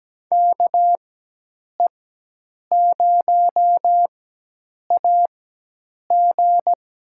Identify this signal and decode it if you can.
Morse code, heard as a keyed tone, about 17 wpm, spaced as for 8 wpm, 700 Hz. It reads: KE0AG
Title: USA Call Signs - 4x Character Spacing - 17wpm